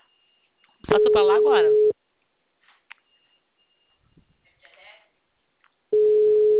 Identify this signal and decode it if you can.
Ringback tone (Europe, 425 Hz), with speech.